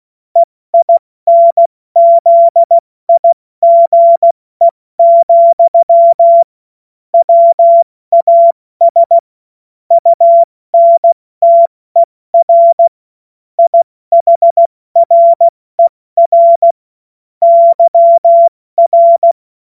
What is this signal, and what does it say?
Morse code, heard as a keyed tone, about 16 wpm, 680 Hz. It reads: EINZIGE, WAS UNTER IHRER YR
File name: Morsekurs / Witze / Witz 27. Juni 2023 70 F80